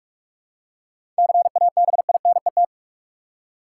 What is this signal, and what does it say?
Morse code, heard as a keyed tone, about 43 wpm, 700 Hz. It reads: CABINET
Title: Seven Letter Words - 45wpm